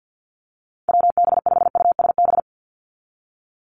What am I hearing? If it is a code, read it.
Morse code, heard as a keyed tone, about 48 wpm, 700 Hz. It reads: WB5USB
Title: NA SOTA Chaser Callsigns 50wpm